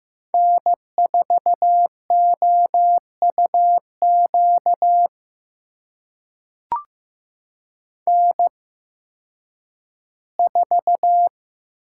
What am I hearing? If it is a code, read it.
Morse code, heard as a keyed tone, about 15 wpm, spaced as for 10 wpm, 700 Hz. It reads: N4OUQ N4
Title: USA Call Signs - 5 Characters - 8x Character Spacing - 15wpm